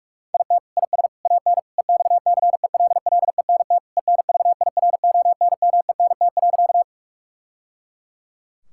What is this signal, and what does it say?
Morse code, heard as a keyed tone, about 46 wpm, 690 Hz. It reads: IT IS AN EXCELLENT ENVIRONMENT.
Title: Sentences from Top 1000 Words - Review Part 5 - 45wpm